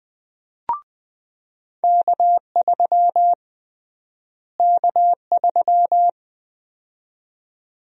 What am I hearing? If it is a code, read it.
Morse code, heard as a keyed tone, about 20 wpm, 700 Hz. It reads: K3 K3
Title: USA Call Sign Prefixes - Repeated 2x - 20wpm